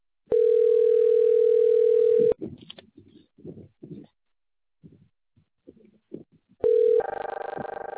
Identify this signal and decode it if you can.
Ringback tone (North America), 440 + 480 Hz.